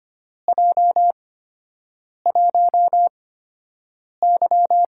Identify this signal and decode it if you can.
Morse code, heard as a keyed tone, about 25 wpm, spaced as for 7 wpm, 700 Hz. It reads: J1Y
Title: USA Call Signs - 8x Character Spacing - 25wpm